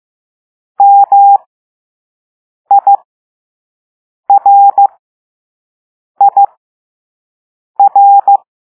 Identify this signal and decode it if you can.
Morse code, heard as a keyed tone, about 15 wpm, spaced as for 5 wpm, 800 Hz. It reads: MIRIR
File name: Morse / Level2A